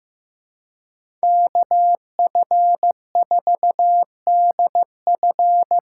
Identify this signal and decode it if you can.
Morse code, heard as a keyed tone, about 15 wpm, 700 Hz. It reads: KF4DF
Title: USA Call Signs - 6x Character Spacing - 15wpm